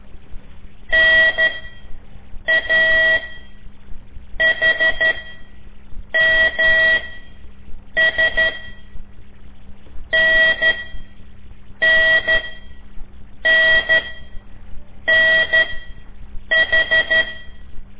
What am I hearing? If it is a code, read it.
Morse code, heard as a keyed tone, about 12 wpm, spaced as for 6 wpm, 1950 Hz. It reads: NAHMS NNNNH
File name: ik1weg2 / CORSI CW / lez5-1